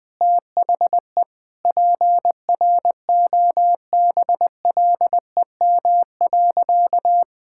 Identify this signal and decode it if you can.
Morse code, heard as a keyed tone, about 20 wpm, 700 Hz. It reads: THE PROBLEM.
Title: Sentences with 4-Word Phrases - Spoken Beginning - 20wpm